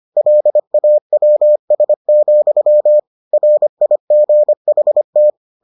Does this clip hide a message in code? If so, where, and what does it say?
Yes: Morse code from 0.2 to 5.3 s, heard as a keyed tone, about 25 wpm, 600 Hz. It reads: LAWS, RIGHT